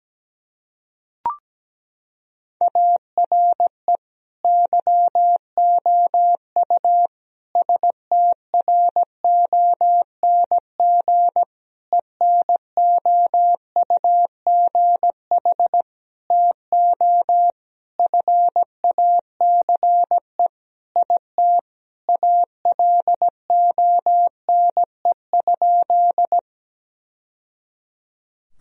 Morse code, heard as a keyed tone, about 17 wpm, 700 Hz. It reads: ARE YOU STRONG ENOUGH TO FACE IT ALONE?
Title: Sentences from Top 1000 Words - Review Part 3 - 17wpm